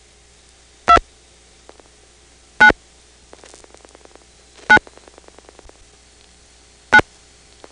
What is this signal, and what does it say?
DTMF, heard as telephone touch tones, tones of about 80 ms. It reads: 2 9 9 9
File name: phonetrips / PrinceWmCoVA5HQ